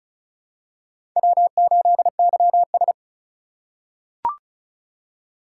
Morse code, heard as a keyed tone, about 36 wpm, 700 Hz. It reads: W8YS
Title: USA Call Signs - 4 Characters - 3x Character Spacing - 35wpm